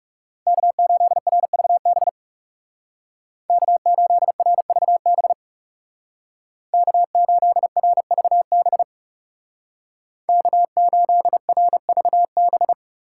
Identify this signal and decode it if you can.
Morse code, heard as a keyed tone, about 36 wpm, 700 Hz. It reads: K8RVB K8RVB K8RVB K8RVB
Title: Speed Racing - USA 1x3 Call Signs 30wpm